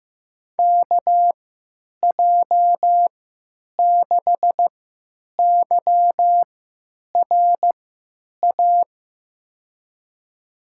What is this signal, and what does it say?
Morse code, heard as a keyed tone, about 16 wpm, spaced as for 9 wpm, 700 Hz. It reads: KJ6YRA
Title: USA Call Signs - 6 Characters - 3x Character Spacing - 15wpm